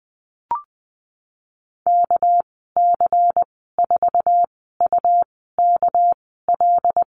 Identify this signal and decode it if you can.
Morse code, heard as a keyed tone, about 20 wpm, spaced as for 14 wpm, 700 Hz. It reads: KC4UKL
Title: 245 Lesson 39g - Letter z - Call Signs 20wpm